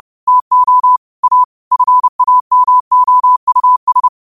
Morse code, heard as a keyed tone, about 30 wpm, 1000 Hz. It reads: TO A FAMOUS
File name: qotdfiles / 30wpm